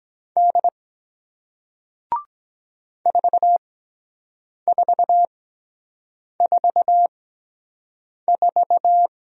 Morse code, heard as a keyed tone, about 23 wpm, spaced as for 7 wpm, 700 Hz. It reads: D 4444